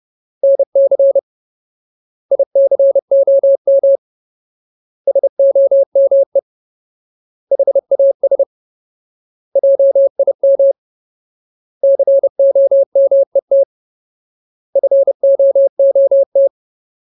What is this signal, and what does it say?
Morse code, heard as a keyed tone, about 30 wpm, 550 Hz. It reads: NC ICOM SOME HAS JIM COMET FOOT
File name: Ultimate Word List / 30WPM Ultimate